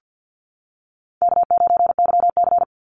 Morse code, heard as a keyed tone, about 50 wpm, 700 Hz. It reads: K9YC
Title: Easy Call Signs - 50wpm